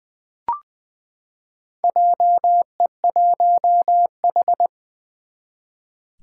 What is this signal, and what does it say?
Morse code, heard as a keyed tone, about 20 wpm, 700 Hz. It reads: JE1H
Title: Four Challenging Letter-Number Combinations 20wpm